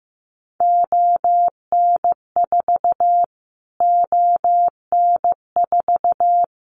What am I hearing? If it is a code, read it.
Morse code, heard as a keyed tone, about 15 wpm, 700 Hz. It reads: ON4 ON4
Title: Callsign Prefixes for Top 25 DXCC - Repeated 3x - 15wpm